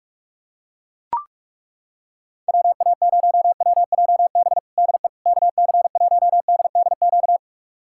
Morse code, heard as a keyed tone, about 41 wpm, 700 Hz. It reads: WA0WJB DE KC1DDQ
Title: USA Callsign DE Callsign - 6 Characters - 45wpm